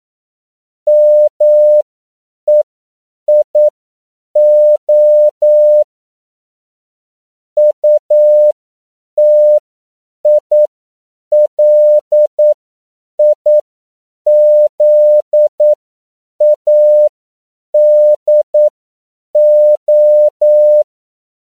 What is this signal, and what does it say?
Morse code, heard as a keyed tone, about 9 wpm, spaced as for 7 wpm, 600 Hz. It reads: MEIO UTILIZADO
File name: CW / 10PPM